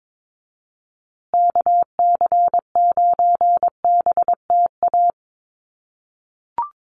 Morse code, heard as a keyed tone, about 22 wpm, 700 Hz. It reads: KC9BTA
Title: USA Call Signs Repeated 2x 22wpm